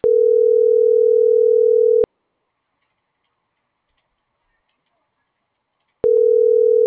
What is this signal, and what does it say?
Ringback tone (North America), 440 + 480 Hz.